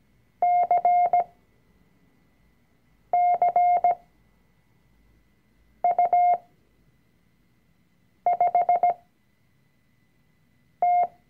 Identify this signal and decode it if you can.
Morse code, heard as a keyed tone, about 17 wpm, spaced as for 4 wpm, 710 Hz. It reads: CCU5T
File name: CW-Online-Kurs / Anfänger / Lektion 004